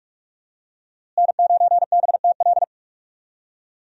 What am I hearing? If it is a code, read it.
Morse code, heard as a keyed tone, about 41 wpm, 700 Hz. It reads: N9BTL
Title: USA Call Signs - 5 Characters - Repeated 3x - 45wpm